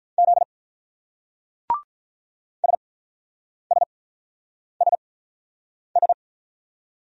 Morse code, heard as a keyed tone, about 44 wpm, spaced as for 9 wpm, 700 Hz. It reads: Z SSSS